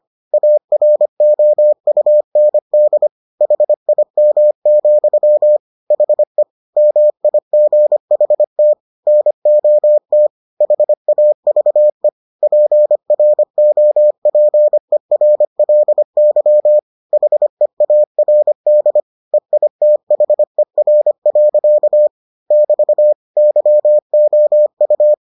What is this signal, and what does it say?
Morse code, heard as a keyed tone, about 25 wpm, 600 Hz. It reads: AROUND HIM, HE MIGHT NOT HAVE PROPERLY HEARD EITHER. <BT> YOU